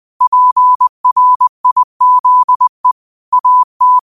Morse code, heard as a keyed tone, about 20 wpm, 1000 Hz. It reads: PRIZE AT
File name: qotdfiles / 20wpm